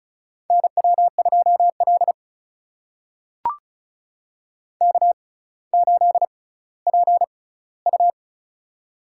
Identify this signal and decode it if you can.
Morse code, heard as a keyed tone, about 34 wpm, spaced as for 18 wpm, 700 Hz. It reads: NW2L K8PU
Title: USA Call Signs - 4 Characters - 6x Character Spacing - 35wpm